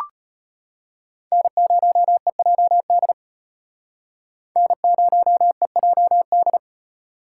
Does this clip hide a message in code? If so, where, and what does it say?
Yes: Morse code from 1.3 to 6.6 s, heard as a keyed tone, about 33 wpm, 700 Hz. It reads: N0EJD N0EJD